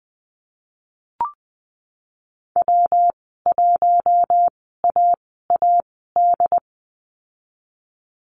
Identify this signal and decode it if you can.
Morse code, heard as a keyed tone, about 20 wpm, spaced as for 15 wpm, 700 Hz. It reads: W1AAD
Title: 122 Lesson 20g - Number 3 - Call Signs 20wpm